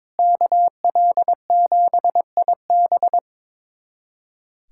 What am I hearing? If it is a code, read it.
Morse code, heard as a keyed tone, about 22 wpm, 700 Hz. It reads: KL7IB
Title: USA Callsign DE Callsign 22wpm